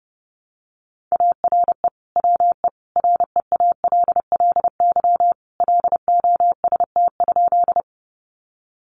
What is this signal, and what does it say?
Morse code, heard as a keyed tone, about 30 wpm, 700 Hz. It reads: ARE WE REALLY LOST?
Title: Sentences from Top 1400 Words 30wpm